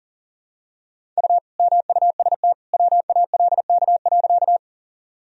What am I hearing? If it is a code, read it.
Morse code, heard as a keyed tone, about 40 wpm, 700 Hz. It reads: U MUST WALK.
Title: Sentences from Top 300 Words and Abbreviations 40wpm